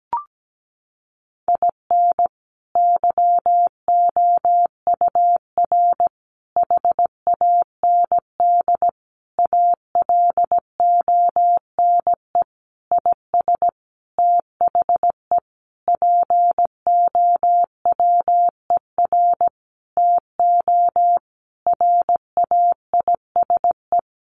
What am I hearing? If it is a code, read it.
Morse code, heard as a keyed tone, about 17 wpm, 700 Hz. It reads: IN YOUR HAND ALONE IS THE POWER TO RAISE
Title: Sentences from Top 1400 Words 17wpm